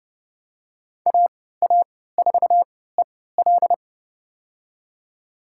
Morse code, heard as a keyed tone, about 30 wpm, spaced as for 17 wpm, 700 Hz. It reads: AA4EL
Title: USA Call Signs - 3x Character Spacing - 30wpm